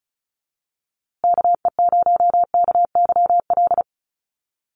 Morse code, heard as a keyed tone, about 34 wpm, 700 Hz. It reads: KE0KYL
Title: USA Call Signs Repeated 3x 35wpm